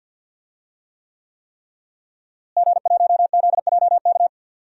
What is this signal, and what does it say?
Morse code, heard as a keyed tone, about 47 wpm, 700 Hz. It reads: G1ZJK
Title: DXCC-100 Call Signs - 5 Characters - Repeated 3x - 50wpm